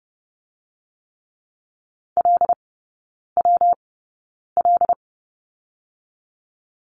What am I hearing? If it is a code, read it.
Morse code, heard as a keyed tone, about 30 wpm, 700 Hz. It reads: L W L
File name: Lesson 19e - Letter w - Three Characters 30wpm